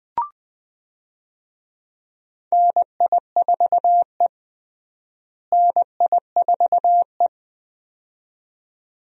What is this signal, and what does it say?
Morse code, heard as a keyed tone, about 20 wpm, 700 Hz. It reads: NI4E NI4E
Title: USA Call Signs - 4 Characters - Repeated 2x - 20wpm